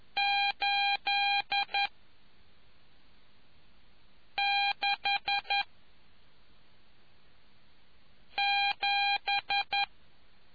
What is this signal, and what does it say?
Morse code, heard as a keyed tone, about 11 wpm, 800 Hz. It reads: 8 6 7